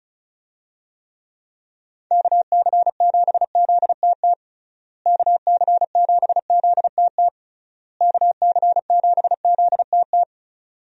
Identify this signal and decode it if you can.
Morse code, heard as a keyed tone, about 36 wpm, 700 Hz. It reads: KC7ZTT KC7ZTT KC7ZTT
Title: USA Call Signs - 6 Characters - Repeated 3x - 35wpm